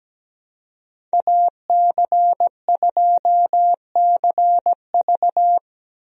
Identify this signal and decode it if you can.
Morse code, heard as a keyed tone, about 17 wpm, 700 Hz. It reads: AC2CV
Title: USA Call Signs - 5 Characters - Repeated 3x - 17wpm